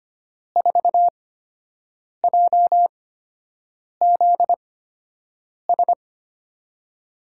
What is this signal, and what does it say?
Morse code, heard as a keyed tone, about 25 wpm, spaced as for 7 wpm, 700 Hz. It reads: 4JZS
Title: USA Call Signs - 5 Characters - 8x Character Spacing - 25wpm